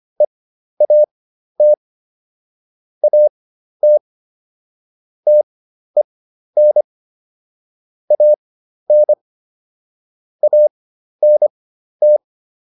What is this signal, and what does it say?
Morse code, heard as a keyed tone, about 25 wpm, spaced as for 12 wpm, 600 Hz. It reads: EAT AT TEN AN ANT